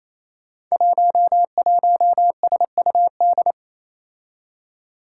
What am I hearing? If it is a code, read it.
Morse code, heard as a keyed tone, about 28 wpm, 700 Hz. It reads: 11SUD
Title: Five Challenging Letter-Number Combinations 28wpm